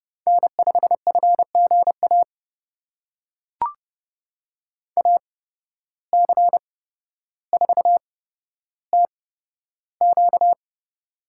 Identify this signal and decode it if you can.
Morse code, heard as a keyed tone, about 30 wpm, spaced as for 13 wpm, 700 Hz. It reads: N5FGA AC4TQ
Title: USA Call Signs - 5 Characters - 8x Character Spacing - 30wpm